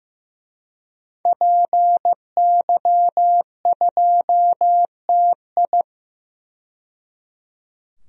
Morse code, heard as a keyed tone, about 15 wpm, 700 Hz. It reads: PY2TI
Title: DXCC-100 Call Signs - 5 Characters - 15wpm